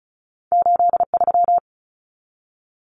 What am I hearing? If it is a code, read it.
Morse code, heard as a keyed tone, about 36 wpm, 700 Hz. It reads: 83
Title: Common Cut Numbers - 2-Digit 35wpm